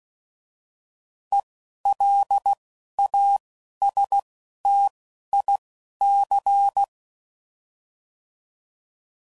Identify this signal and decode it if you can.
Morse code, heard as a keyed tone, about 16 wpm, spaced as for 11 wpm, 800 Hz. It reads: ELASTIC